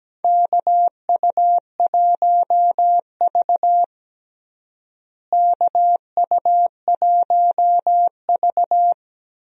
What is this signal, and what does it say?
Morse code, heard as a keyed tone, about 17 wpm, 700 Hz. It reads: KU1V KU1V
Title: USA Call Signs - 4 Characters - Repeated 3x - 17wpm